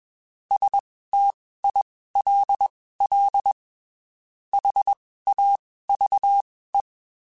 Morse code, heard as a keyed tone, about 21 wpm, spaced as for 15 wpm, 800 Hz. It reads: STILL HAVE
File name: (spaced) / QSOs Spaced-mp3s / WA7QOX 1 @20 sp